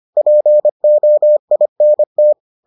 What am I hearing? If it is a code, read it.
Morse code, heard as a keyed tone, about 25 wpm, 600 Hz. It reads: POINT